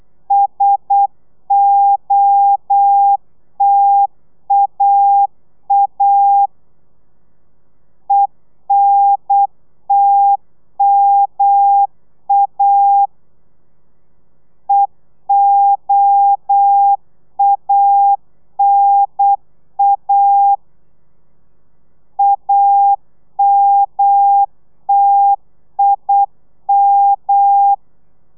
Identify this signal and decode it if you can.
Morse code, heard as a keyed tone, about 8 wpm, 800 Hz. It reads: SOTAA ENTMA EOANA AMTIM